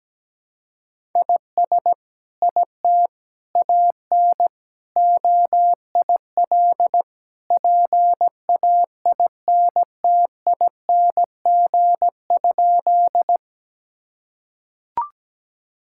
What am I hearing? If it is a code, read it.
Morse code, heard as a keyed tone, about 17 wpm, 700 Hz. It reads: IS IT AN OIL PAINTING?
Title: Sentences from Top 1000 Words - Review Part 6 - 17wpm